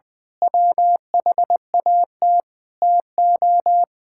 Morse code, heard as a keyed tone, about 20 wpm, 700 Hz. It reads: WHAT TO
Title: Sentences with 4-Word Phrases - 20wpm